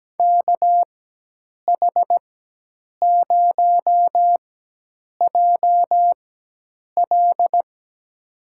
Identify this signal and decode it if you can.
Morse code, heard as a keyed tone, about 17 wpm, spaced as for 8 wpm, 700 Hz. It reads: KH0JL
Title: USA Call Signs - 4x Character Spacing - 17wpm